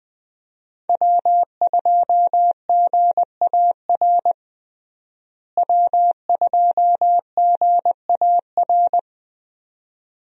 Morse code, heard as a keyed tone, about 20 wpm, 700 Hz. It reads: W2GAR W2GAR
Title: USA Call Signs - 5 Characters - Repeated 2x - 20wpm